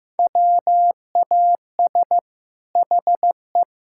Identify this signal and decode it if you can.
Morse code, heard as a keyed tone, about 15 wpm, 700 Hz. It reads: WAS HE
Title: Sentences from Top 1000 Words - Review Part 6 - 15wpm